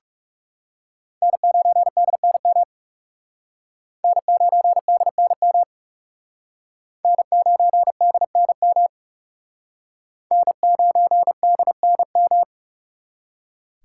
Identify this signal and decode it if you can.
Morse code, heard as a keyed tone, about 38 wpm, 700 Hz. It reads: N9DNM N9DNM N9DNM N9DNM